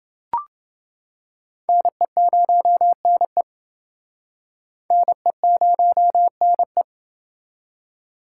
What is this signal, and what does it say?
Morse code, heard as a keyed tone, about 28 wpm, 700 Hz. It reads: NE0NE NE0NE